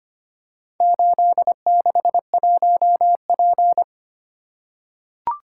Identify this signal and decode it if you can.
Morse code, heard as a keyed tone, about 25 wpm, 700 Hz. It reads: 861P
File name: practice / 25wpm / Four Letter-Number Combinations - 25wpm